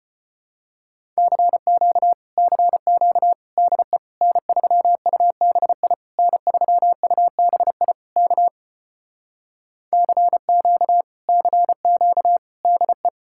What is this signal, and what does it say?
Morse code, heard as a keyed tone, about 31 wpm, 700 Hz. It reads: CQ CQ DE N3UBI N3UBI K CQ CQ DE
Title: Speed-Racing Common QSO Phrases - 25wpm